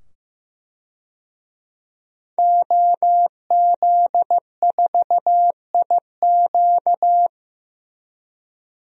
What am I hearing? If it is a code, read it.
Morse code, heard as a keyed tone, about 15 wpm, 700 Hz. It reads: OZ4IQ
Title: DXCC-100 Call Signs - 5 Characters - 15wpm